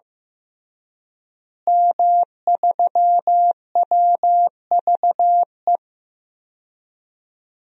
Morse code, heard as a keyed tone, about 15 wpm, 700 Hz. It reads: M3WVE
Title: DXCC-100 Call Signs - 5 Characters - Repeated 3x - 15wpm